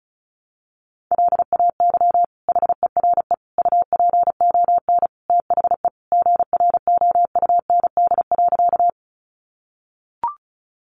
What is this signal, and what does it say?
Morse code, heard as a keyed tone, about 34 wpm, 700 Hz. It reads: LAY HERE UPON THE GROUND.
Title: Sentences from Top 1100 Words 35wpm